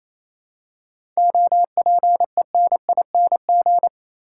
Morse code, heard as a keyed tone, about 27 wpm, 700 Hz. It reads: OPENING
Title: Seven Letter Words - 28wpm